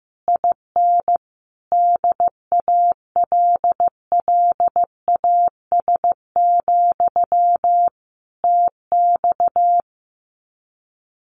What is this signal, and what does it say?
Morse code, heard as a keyed tone, about 14 wpm, 700 Hz. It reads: IN DALLAS, TX
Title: Common QSO Phrases Encore 15wpm